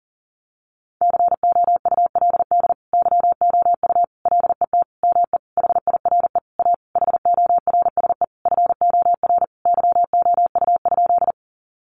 Morse code, heard as a keyed tone, about 40 wpm, 700 Hz. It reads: COULD YOU LET ME HIRE A HORSE FOR YOU?
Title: Sentences from Top 1400 Words 40wpm